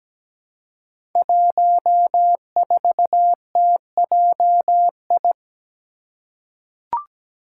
Morse code, heard as a keyed tone, about 17 wpm, 700 Hz. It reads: 14TJI
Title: Five Challenging Letter-Number Combinations 17wpm